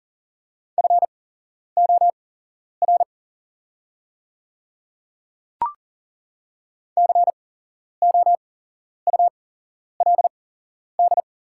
Morse code, heard as a keyed tone, about 38 wpm, spaced as for 11 wpm, 700 Hz. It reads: FOR COULD
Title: 8x Character Spacing - Top 100 Words - 40wpm